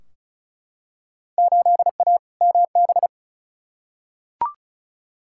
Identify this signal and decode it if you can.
Morse code, heard as a keyed tone, about 36 wpm, 690 Hz. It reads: 8A MB
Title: Field Day Exchanges Encore - 35wpm